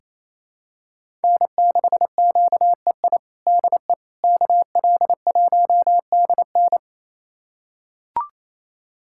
Morse code, heard as a keyed tone, about 28 wpm, 700 Hz. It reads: N6QEI DE KL1DN
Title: USA Callsign DE Callsign - 5 Characters - 28wpm